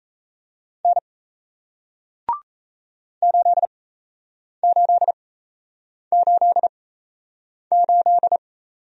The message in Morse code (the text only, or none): N 8888